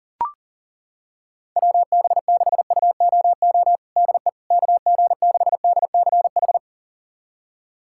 Morse code, heard as a keyed tone, about 40 wpm, 700 Hz. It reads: WB6UOO DE KG6DCH
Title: USA Callsign DE Callsign 40wpm